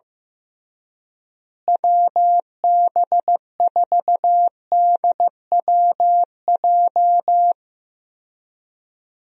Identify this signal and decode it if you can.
Morse code, heard as a keyed tone, about 15 wpm, 700 Hz. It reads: WB4DWJ
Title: USA Call Signs - 6 Characters - Repeated 2x - 15wpm